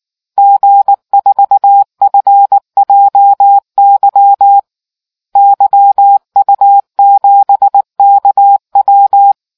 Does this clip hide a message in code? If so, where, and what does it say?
Yes: Morse code from 0.4 to 9.3 s, heard as a keyed tone, about 19 wpm, 800 Hz. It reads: G4FJY YU7KW